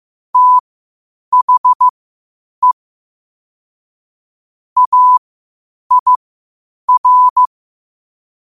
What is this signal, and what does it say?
Morse code, heard as a keyed tone, about 15 wpm, spaced as for 8 wpm, 1000 Hz. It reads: THE AIR
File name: qotdfiles / 8wpm